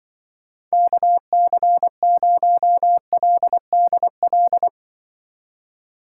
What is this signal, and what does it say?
Morse code, heard as a keyed tone, about 24 wpm, 700 Hz. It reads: KC0LDL
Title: Speed Racing - USA Call Signs 20wpm